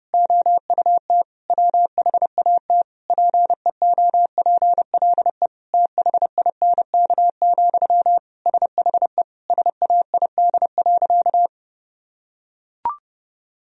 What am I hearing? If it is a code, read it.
Morse code, heard as a keyed tone, about 30 wpm, 700 Hz. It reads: OUT WHAT PEOPLE THINK, SHE SAID.
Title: Sentences from Top 1000 Words - Review Part 4 - 30wpm